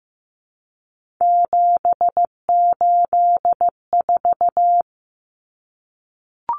Morse code, heard as a keyed tone, about 15 wpm, 700 Hz. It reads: 784
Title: Numbers - 3-Digit 15wpm